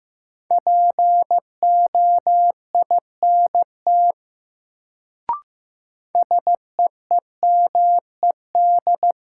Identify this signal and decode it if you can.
Morse code, heard as a keyed tone, about 15 wpm, 700 Hz. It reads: POINT SEEMED
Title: Sets of 3 Words - New Word Drill - 15wpm